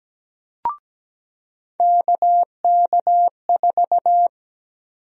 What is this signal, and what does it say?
Morse code, heard as a keyed tone, about 17 wpm, 700 Hz. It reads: KK4
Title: USA Call Sign Prefixes - 17wpm